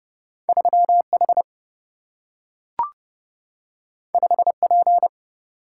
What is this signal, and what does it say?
Morse code, heard as a keyed tone, about 30 wpm, 700 Hz. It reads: 3 H 5 P
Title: Single-Letter Single-Number 30wpm